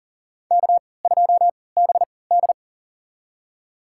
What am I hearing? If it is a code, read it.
Morse code, heard as a keyed tone, about 40 wpm, spaced as for 23 wpm, 700 Hz. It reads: K2BD
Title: USA Call Signs - 4 Characters - 3x Character Spacing - 40wpm